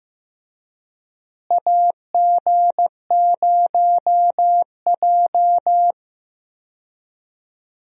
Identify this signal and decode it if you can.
Morse code, heard as a keyed tone, about 15 wpm, 700 Hz. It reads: AG0J
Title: USA Call Signs - 4 Characters - Repeated 2x - 15wpm